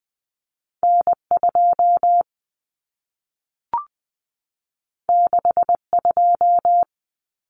Morse code, heard as a keyed tone, about 20 wpm, 700 Hz. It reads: N2 62
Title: Common Cut Numbers - 2-Digit 20wpm